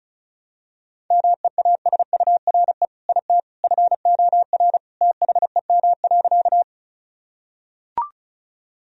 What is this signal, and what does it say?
Morse code, heard as a keyed tone, about 36 wpm, 700 Hz. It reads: MEASURE IT FOR THEM.
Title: Sentences from Top 1000 Words - Review Part 2 - 35wpm